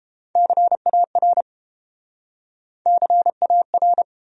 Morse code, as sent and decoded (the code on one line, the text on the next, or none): -.-. .- .-. / -.-. .- .-.
CAR CAR